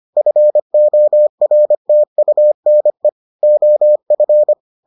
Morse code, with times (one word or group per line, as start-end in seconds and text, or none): 0.2-3.1 FORTUNE
3.4-4.5 OF